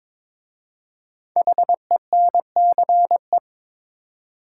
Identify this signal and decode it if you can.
Morse code, heard as a keyed tone, about 22 wpm, 700 Hz. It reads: HENCE
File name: 5 Letter Words - First 3 Letters Given - 22wpm